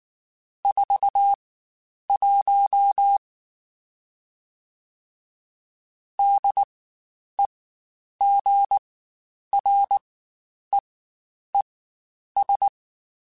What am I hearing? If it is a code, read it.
Morse code, heard as a keyed tone, about 19 wpm, spaced as for 9 wpm, 800 Hz. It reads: 41 DEGREES
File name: Spaced Characters / Spaced Characters-mp3s / Weather 4 14 sp @18